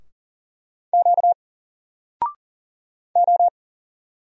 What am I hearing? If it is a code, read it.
Morse code, heard as a keyed tone, about 33 wpm, spaced as for 5 wpm, 700 Hz. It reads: QO